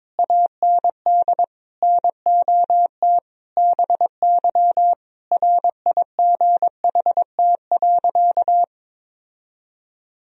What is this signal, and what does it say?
Morse code, heard as a keyed tone, about 23 wpm, 700 Hz. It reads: AND NOT BY RIGHT.
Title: Sentences from Top 1000 Words - Review Part 3 - 22wpm